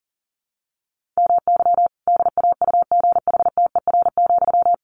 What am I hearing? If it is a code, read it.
Morse code, heard as a keyed tone, about 40 wpm, 700 Hz. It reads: MY DAUGHTER,
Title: Sentences from Top 1500 Words 40wpm